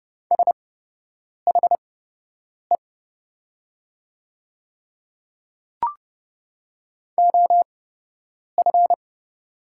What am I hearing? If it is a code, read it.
Morse code, heard as a keyed tone, about 30 wpm, spaced as for 8 wpm, 700 Hz. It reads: SHE OF
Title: 8x Character Spacing - Top 100 Words - 30wpm